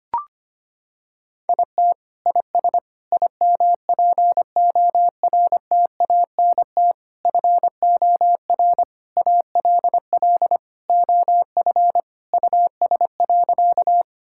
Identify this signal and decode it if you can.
Morse code, heard as a keyed tone, about 25 wpm, 700 Hz. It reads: IT IS IMPORTANT FOR ALL OF US.
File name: practice / 25wpm / Sentences with 4-Word Phrases - 25wpm